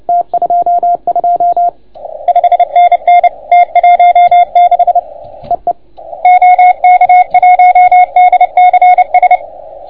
Morse code, heard as a keyed tone, about 28 wpm, 690 Hz, with radio noise. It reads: T22 5NN T16 EE OK1DCS